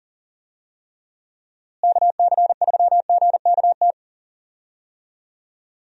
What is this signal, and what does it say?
Morse code, heard as a keyed tone, about 37 wpm, 700 Hz. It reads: KC3GKT
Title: USA Call Signs - 6 Characters - Repeated 3x - 40wpm